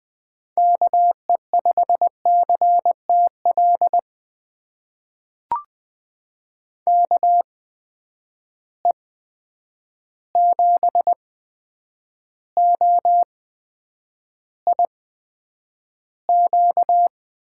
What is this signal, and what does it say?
Morse code, heard as a keyed tone, about 21 wpm, spaced as for 9 wpm, 700 Hz. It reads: KE5CTL KE7OIQ
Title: USA Call Signs - 6 Characters - 8x Character Spacing - 20wpm